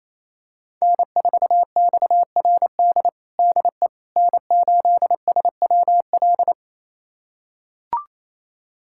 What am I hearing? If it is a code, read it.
Morse code, heard as a keyed tone, about 28 wpm, 700 Hz. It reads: N4XRD DE N8SWL
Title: USA Callsign DE Callsign 28wpm